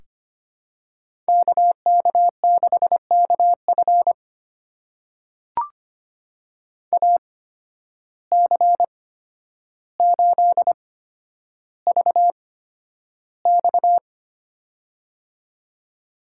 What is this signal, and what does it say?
Morse code, heard as a keyed tone, about 25 wpm, spaced as for 11 wpm, 700 Hz. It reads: KK6KF AC8VX
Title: USA Call Signs - 5 Characters - 8x Character Spacing - 25wpm